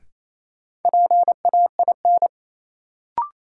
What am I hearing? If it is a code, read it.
Morse code, heard as a keyed tone, about 28 wpm, 700 Hz. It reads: PAIN
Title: Sentences from Top 1600 Words - New Word Drill Alt - 28wpm